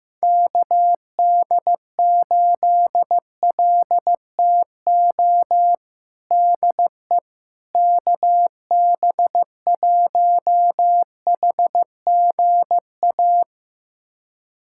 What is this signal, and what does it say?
Morse code, heard as a keyed tone, about 16 wpm, 700 Hz. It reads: KD8LTO DE KB1HGA
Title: USA Callsign DE Callsign - 6 Characters - 15wpm